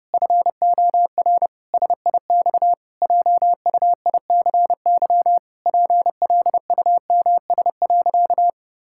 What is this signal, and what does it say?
Morse code, heard as a keyed tone, about 30 wpm, 700 Hz. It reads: FOR SIX JUICY PLUMS.
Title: Pangram Challenge v2 - 30wpm